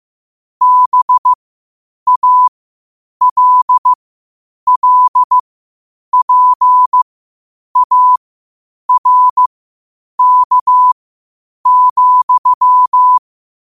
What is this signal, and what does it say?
Morse code, heard as a keyed tone, about 15 wpm, spaced as for 8 wpm, 1000 Hz. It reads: BALLPARK,
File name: qotdfiles / 8wpm